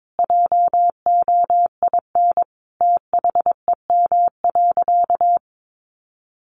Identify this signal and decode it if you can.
Morse code, heard as a keyed tone, about 22 wpm, 700 Hz. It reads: JOIN THEM.
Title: Sentences from Top 1400 Words 22wpm